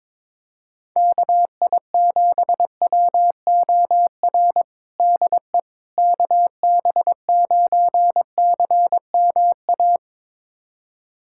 Morse code, heard as a keyed tone, about 22 wpm, 700 Hz. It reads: KI7WOR DE KB9CMA
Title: USA Callsign DE Callsign - 6 Characters - 22wpm